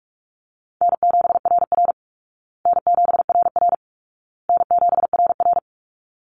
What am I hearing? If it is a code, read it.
Morse code, heard as a keyed tone, about 44 wpm, 700 Hz. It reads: N7RR N7RR N7RR
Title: Easy Call Signs Repeated 3x - 45wpm